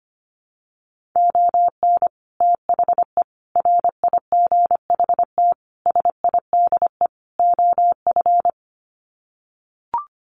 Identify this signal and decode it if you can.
Morse code, heard as a keyed tone, about 25 wpm, 700 Hz. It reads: ON THE RIGHT SIDE OF
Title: Sets of 5 Words Encore 25wpm